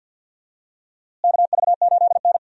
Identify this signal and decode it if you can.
Morse code, heard as a keyed tone, about 46 wpm, 700 Hz. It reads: KV8N
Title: USA Call Signs - 4 Characters - Repeated 3x - 50wpm